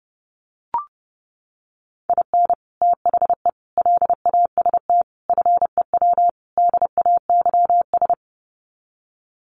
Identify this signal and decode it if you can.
Morse code, heard as a keyed tone, about 30 wpm, 700 Hz. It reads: IN THE LAST FEW DAYS
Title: Sets of 5 Words Encore2 30wpm